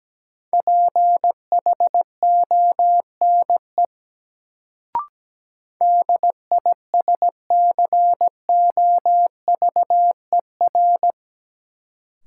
Morse code, heard as a keyed tone, about 17 wpm, 700 Hz. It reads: PHONE DISCOVER
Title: Sentences from Top 1100 Words - New Word Drill - 17wpm